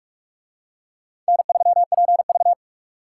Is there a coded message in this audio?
Yes: Morse code from 1.3 to 2.5 s, heard as a keyed tone, about 43 wpm, 700 Hz. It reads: N3PV